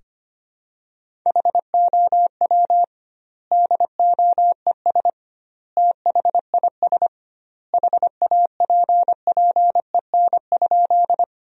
Morse code, heard as a keyed tone, about 25 wpm, 700 Hz. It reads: HOW DOES THIS HAPPEN?
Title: 2x Word-Spacing - Sentences from Top 400 Words - 25wpm